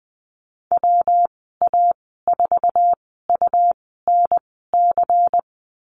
Morse code, heard as a keyed tone, about 20 wpm, spaced as for 15 wpm, 700 Hz. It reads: WA4UNC
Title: 161 Lesson 26g - Letter g - Call Signs 20wpm